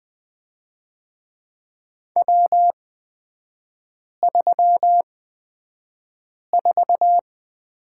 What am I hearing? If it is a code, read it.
Morse code, heard as a keyed tone, about 20 wpm, spaced as for 5 wpm, 700 Hz. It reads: W34
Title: 236 Lesson 38e - Letter q - Three Characters 20wpm